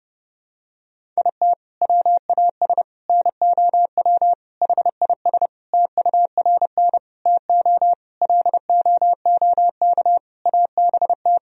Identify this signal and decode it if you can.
Morse code, heard as a keyed tone, about 32 wpm, 700 Hz. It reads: IT WAS NOW HIS TURN TO LOOK ABT
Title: Sentences from Top 200 Words and Abbreviations 30wpm